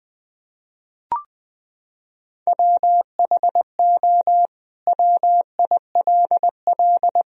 Morse code, heard as a keyed tone, about 20 wpm, 700 Hz. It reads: WHO WILL
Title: Sentences from Top 1000 Words - Review Part 3 - 20wpm